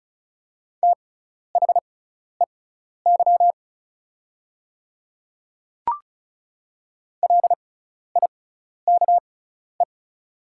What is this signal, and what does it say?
Morse code, heard as a keyed tone, about 34 wpm, spaced as for 12 wpm, 700 Hz. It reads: THEY LIKE
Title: 6x Character Spacing - Top 100 Words - 35wpm